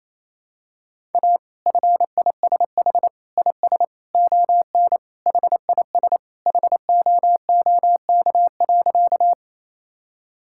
Morse code, heard as a keyed tone, about 27 wpm, 700 Hz. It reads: A FISH IS ON HIS HOOK.
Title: Sentences from Top 1600 Words 28wpm